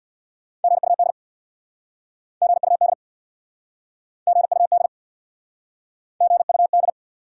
Morse code, heard as a keyed tone, about 69 wpm, 700 Hz. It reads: GUD GUD GUD GUD